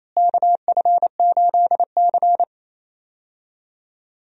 Morse code, heard as a keyed tone, about 28 wpm, 700 Hz. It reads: KF8C
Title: USA Call Signs - 4 Characters - Repeated 3x - 28wpm